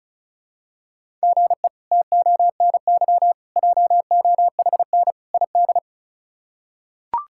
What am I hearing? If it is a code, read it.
Morse code, heard as a keyed tone, about 36 wpm, 700 Hz. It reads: GE TONY JOHN ID
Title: K1USN SST QSO Practice - 35wpm